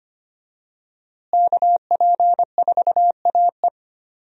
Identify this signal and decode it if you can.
Morse code, heard as a keyed tone, about 25 wpm, 700 Hz. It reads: KP4AE
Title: USA Call Signs - 5 Characters - Repeated 2x - 25wpm